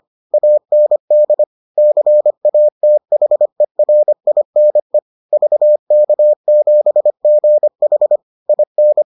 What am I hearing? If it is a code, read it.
Morse code, heard as a keyed tone, about 25 wpm, 600 Hz. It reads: AND CATHERINE VK7GH IN